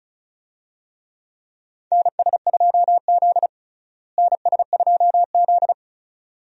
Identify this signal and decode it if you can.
Morse code, heard as a keyed tone, about 34 wpm, 700 Hz. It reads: NS2Z NS2Z
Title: USA Call Signs - 4 Characters - Repeated 2x - 35wpm